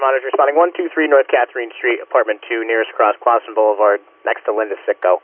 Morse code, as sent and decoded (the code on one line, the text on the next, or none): none